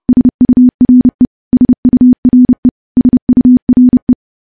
Morse code, heard as a keyed tone, about 30 wpm, 250 Hz. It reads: SURE SURE SURE
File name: morse / morse-training-30WPM